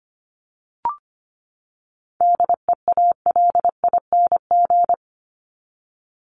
Morse code, heard as a keyed tone, about 25 wpm, 700 Hz. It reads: DEALING
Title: ING Suffix - Top 100 Words with Base Words 25wpm